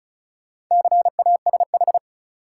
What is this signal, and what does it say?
Morse code, heard as a keyed tone, about 36 wpm, 700 Hz. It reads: CASH
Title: Sentences from Top 700 Words - New Word Drill - 35wpm